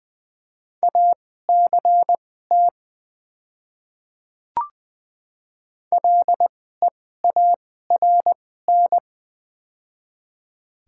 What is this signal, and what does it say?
Morse code, heard as a keyed tone, about 20 wpm, spaced as for 14 wpm, 700 Hz. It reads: ACT LEARN
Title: 2x Character Spacing - Top 500 Words - 20wpm